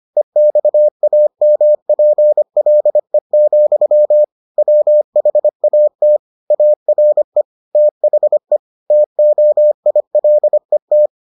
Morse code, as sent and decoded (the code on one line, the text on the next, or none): . -..- .- -- .--. .-.. --..-- / .-- .... .- - / .- .-. / - .... / - --- .. .-.. -
EXAMPLE, WHAT ARE THE TOILET